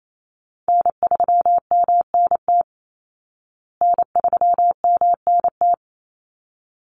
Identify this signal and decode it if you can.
Morse code, heard as a keyed tone, about 28 wpm, 690 Hz. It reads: N3MNT N3MNT
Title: Call Signs Repeated 3x - 28wpm